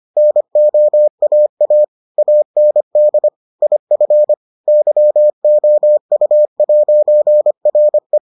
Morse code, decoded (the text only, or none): NOAA AND IF YOU'RE